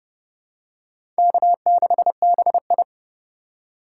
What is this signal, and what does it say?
Morse code, heard as a keyed tone, about 29 wpm, 700 Hz. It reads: K6BI
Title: USA Call Signs - 4 Characters - 2x Character Spacing - 30wpm